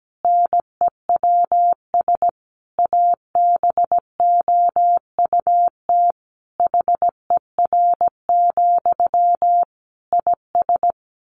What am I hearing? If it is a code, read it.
Morse code, heard as a keyed tone, about 17 wpm, 700 Hz. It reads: NEWS ABOUT HER, IS
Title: Sentences from Top 1500 Words 17wpm